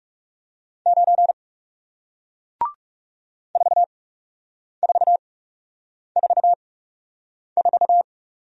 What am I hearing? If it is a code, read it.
Morse code, heard as a keyed tone, about 39 wpm, spaced as for 8 wpm, 700 Hz. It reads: 9 4444